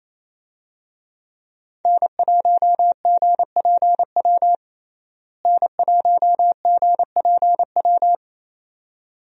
Morse code, heard as a keyed tone, about 29 wpm, 700 Hz. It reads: N1GPW N1GPW
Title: USA Call Signs - 5 Characters - Repeated 2x - 28wpm